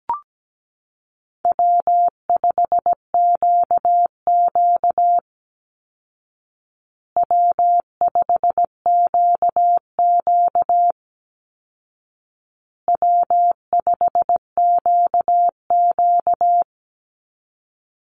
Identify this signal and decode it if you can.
Morse code, heard as a keyed tone, about 17 wpm, 700 Hz. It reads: W5QQ W5QQ W5QQ
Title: Call Signs Repeated 2x - 17wpm